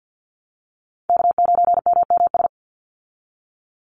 Morse code, heard as a keyed tone, about 50 wpm, 700 Hz. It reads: K9MMS
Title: Call Signs Repeated 2x - 50wpm